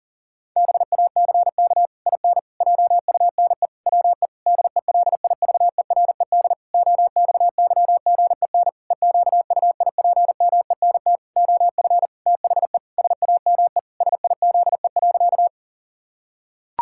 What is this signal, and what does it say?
Morse code, heard as a keyed tone, about 37 wpm, 700 Hz. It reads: BACK IN JUNE WE DELIVERED OXYGEN EQUIPMENT OF THE SAME SIZE.
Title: Pangram Challenge v2 - 40wpm